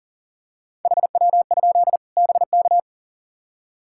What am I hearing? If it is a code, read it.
Morse code, heard as a keyed tone, about 38 wpm, 700 Hz. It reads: HW? BK
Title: K1USN SST QSO Practice 40wpm